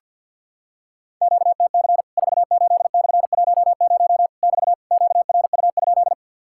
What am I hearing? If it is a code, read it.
Morse code, heard as a keyed tone, about 53 wpm, 700 Hz. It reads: QTC 48/10 <BT> QRU?
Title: Worked All Europe DX Contest - QTC Exchange - 50wpm